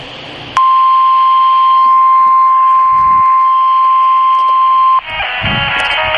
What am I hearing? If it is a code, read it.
Emergency Alert System (SAME) tones (the 1050 Hz attention tone). It is heard from 0.5 to 5.0 s.
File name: RR Skywarn / Washtenaw County SKYWARN N8DUY and W8SRC Repeaters